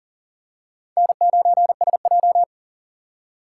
Morse code, heard as a keyed tone, about 37 wpm, 700 Hz. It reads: N9SJ
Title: USA Call Signs - 4 Characters - 3x Character Spacing - 40wpm